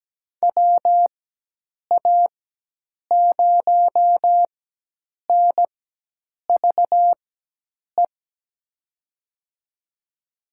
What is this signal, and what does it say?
Morse code, heard as a keyed tone, about 17 wpm, spaced as for 8 wpm, 700 Hz. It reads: WA0NVE
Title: USA Call Signs - 6 Characters - 4x Character Spacing - 17wpm